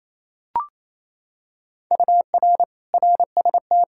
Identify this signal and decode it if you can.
Morse code, heard as a keyed tone, about 28 wpm, 700 Hz. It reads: UR RST
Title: Practice QSO - Exchange 1 - Beginning - 28wpm